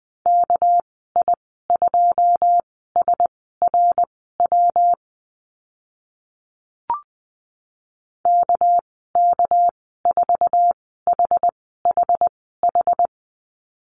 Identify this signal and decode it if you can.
Morse code, heard as a keyed tone, about 20 wpm, spaced as for 14 wpm, 700 Hz. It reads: KI2SRW KK4HHH
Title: 200 Lesson 32g - Letter k - Call Signs 20wpm